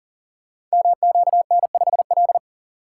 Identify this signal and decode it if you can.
Morse code, heard as a keyed tone, about 38 wpm, 700 Hz. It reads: MQN5L
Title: Code Groups - 5 Letter:Numbers - 40wpm